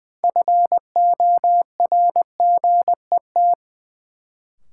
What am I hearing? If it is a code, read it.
Morse code, heard as a keyed tone, about 21 wpm, 700 Hz. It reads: FORGET
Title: Sentences from Top 1200 Words - New Word Drill - 20wpm